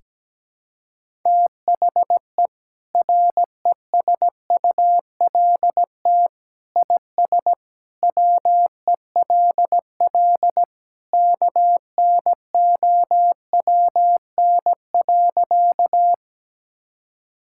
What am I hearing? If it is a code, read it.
Morse code, heard as a keyed tone, about 17 wpm, 700 Hz. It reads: THE RESULT IS WELL KNOWN.
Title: Sentences from Top 1000 Words - Review Part 5 - 17wpm